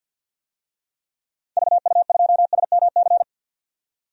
Morse code, heard as a keyed tone, about 53 wpm, 700 Hz. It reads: VU2SMC
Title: DXCC-100 Call Signs - 6 Characters - 50wpm